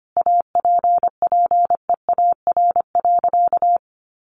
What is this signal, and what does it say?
Morse code, heard as a keyed tone, about 25 wpm, 700 Hz. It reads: APPEAR.
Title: Sentences from Top 1200 Words 25wpm